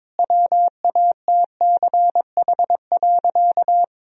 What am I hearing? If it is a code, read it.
Morse code, heard as a keyed tone, about 22 wpm, 700 Hz. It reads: WATCH.
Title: Sentences from Top 1000 Words - Review Part 1 - 22wpm